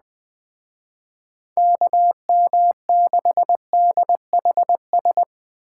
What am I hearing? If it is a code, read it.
Morse code, heard as a keyed tone, about 20 wpm, 700 Hz. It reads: KM6DHS